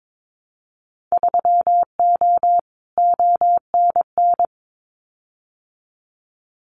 Morse code, heard as a keyed tone, about 22 wpm, 700 Hz. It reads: 3O ONN
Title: Winter Field Day Exchanges 22wpm